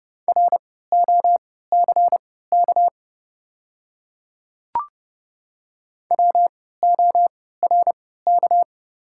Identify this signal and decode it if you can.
Morse code, heard as a keyed tone, about 30 wpm, spaced as for 17 wpm, 690 Hz. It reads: ROCK WORK
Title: 3x Character Spacing - Top 500 Words - 30wpm